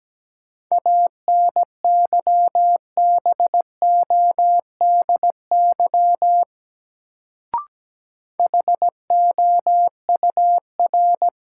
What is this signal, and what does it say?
Morse code, heard as a keyed tone, about 17 wpm, 700 Hz. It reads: ANYBODY HOUR
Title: Sets of 5 Words - New Word Drill - 17wpm